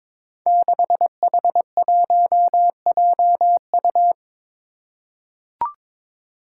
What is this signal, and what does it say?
Morse code, heard as a keyed tone, about 22 wpm, 700 Hz. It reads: 6H1JU
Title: Five Challenging Letter-Number Combinations 22wpm